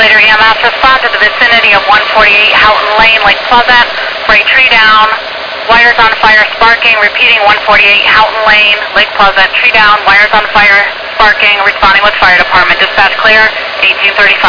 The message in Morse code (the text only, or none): none